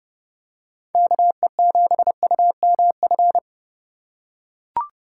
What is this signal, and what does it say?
Morse code, heard as a keyed tone, about 30 wpm, 700 Hz. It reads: KE7UMF